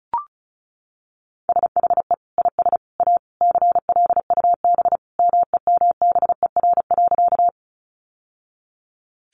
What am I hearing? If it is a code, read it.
Morse code, heard as a keyed tone, about 34 wpm, 700 Hz. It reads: SHE IS A CLUB MEMBER.
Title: Sentences from Top 1200 Words 35wpm